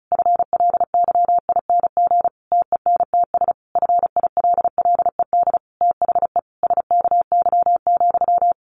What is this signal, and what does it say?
Morse code, heard as a keyed tone, about 34 wpm, 700 Hz. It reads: FLYING TENTS FILLED THE SKY,
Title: A Sound Of Thunder 35wpm